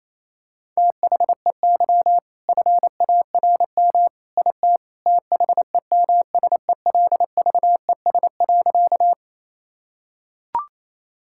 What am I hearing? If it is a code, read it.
Morse code, heard as a keyed tone, about 28 wpm, 700 Hz. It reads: THEY FARM IT THEMSELVES.
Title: Sentences from Top 1000 Words - Review Part 5 - 28wpm